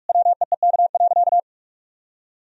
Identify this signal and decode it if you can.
Morse code, heard as a keyed tone, about 44 wpm, 700 Hz. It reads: WEEK.
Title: Sentences from Top 1000 Words - Review Part 2 - 45wpm